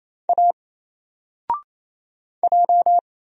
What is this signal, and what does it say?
Morse code, heard as a keyed tone, about 27 wpm, 700 Hz. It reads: A J
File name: Single-Letter-Number-28wpm